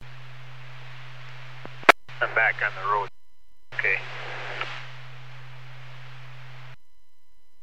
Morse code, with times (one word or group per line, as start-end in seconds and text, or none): none